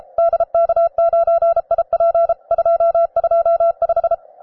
Morse code, heard as a keyed tone, about 31 wpm, 670 Hz. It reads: DK9IP 225
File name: wae / cw advanced / rw9sw cw